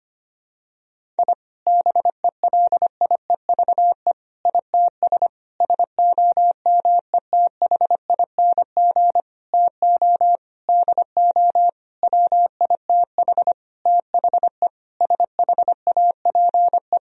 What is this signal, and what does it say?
Morse code, heard as a keyed tone, about 25 wpm, 700 Hz. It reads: I BELIEVE ITS SOMETHING TO DO WITH THE SHAPE
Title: Sentences from Top 1000 Words - Review Part 3 - 25wpm